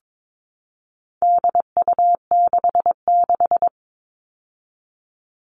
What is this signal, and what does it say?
Morse code, heard as a keyed tone, about 22 wpm, 700 Hz. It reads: DU66
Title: All Cut Numbers - 4-Digit 22wpm